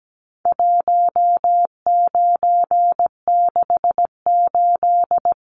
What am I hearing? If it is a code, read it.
Morse code, heard as a keyed tone, about 17 wpm, 700 Hz. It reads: 1968
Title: Numbers - 4-Digit 17wpm